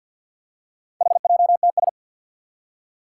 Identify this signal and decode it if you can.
Morse code, heard as a keyed tone, about 48 wpm, 700 Hz. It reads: HJTS
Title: Four Challenging Letter-Number Combinations 50wpm